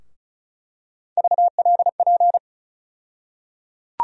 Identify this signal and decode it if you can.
Morse code, heard as a keyed tone, about 36 wpm, 700 Hz. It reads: VLP
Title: USA Call Sign Postfixes - 3 Letters 35wpm